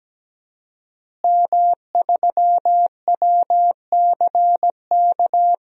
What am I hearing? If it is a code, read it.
Morse code, heard as a keyed tone, about 17 wpm, 700 Hz. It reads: M3WCK
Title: DXCC-100 Call Signs - 5 Characters - Repeated 2x - 17wpm